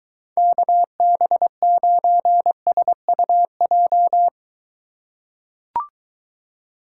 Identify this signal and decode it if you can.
Morse code, heard as a keyed tone, about 23 wpm, 700 Hz. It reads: KB9SUJ